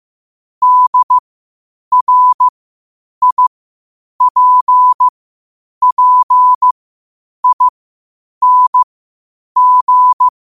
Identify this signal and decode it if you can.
Morse code, heard as a keyed tone, about 15 wpm, spaced as for 8 wpm, 1000 Hz. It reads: DRIPPING
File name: qotdfiles / 8wpm